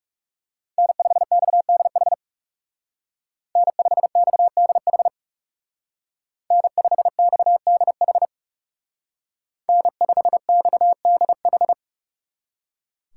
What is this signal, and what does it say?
Morse code, heard as a keyed tone, about 36 wpm, 700 Hz. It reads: N5XDH N5XDH N5XDH N5XDH
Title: Speed Racing - USA 1x3 Call Signs 30wpm